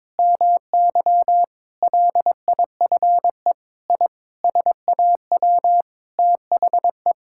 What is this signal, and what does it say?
Morse code, heard as a keyed tone, about 22 wpm, 700 Hz. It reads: MY LIFE I SAW THE
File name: practice / 22wpm / Sentences with 8-Word Phrases - 22wpm